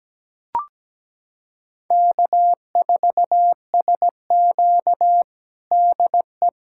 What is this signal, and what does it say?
Morse code, heard as a keyed tone, about 17 wpm, 700 Hz. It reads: K4SQ DE
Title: USA Callsign DE Callsign - 4 Characters - 17wpm